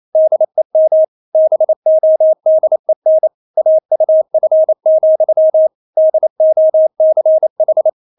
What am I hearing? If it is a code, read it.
Morse code, heard as a keyed tone, about 28 wpm, 630 Hz. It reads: DEM BODEN AUF, DOCH
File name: Witz9 28wpm